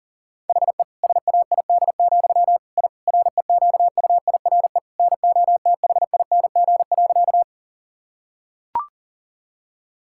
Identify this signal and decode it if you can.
Morse code, heard as a keyed tone, about 40 wpm, 700 Hz. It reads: HE SAID, I REQUIRE NOTHING.
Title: Sentences from Top 1000 Words - Review Part 6 - 40wpm